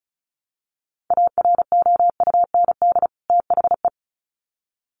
Morse code, heard as a keyed tone, about 34 wpm, 700 Hz. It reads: AROUND THE
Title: Sets of 2 Words 35wpm